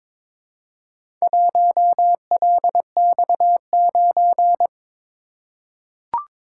Morse code, heard as a keyed tone, about 22 wpm, 700 Hz. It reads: 1LX9
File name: practice / 22wpm / Four Letter-Number Combinations - 22wpm